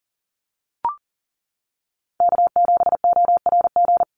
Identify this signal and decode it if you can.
Morse code, heard as a keyed tone, about 40 wpm, 700 Hz. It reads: K7ORG
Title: NA SOTA Chaser Callsigns 40wpm